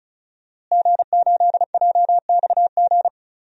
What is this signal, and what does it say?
Morse code, heard as a keyed tone, about 34 wpm, 700 Hz. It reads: G8JXG
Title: DXCC-100 Call Signs - 5 Characters - Repeated 2x - 35wpm